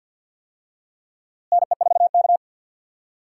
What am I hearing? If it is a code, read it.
Morse code, heard as a keyed tone, about 52 wpm, 700 Hz. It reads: NE4K
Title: USA Call Signs - 4 Characters - Repeated 3x - 50wpm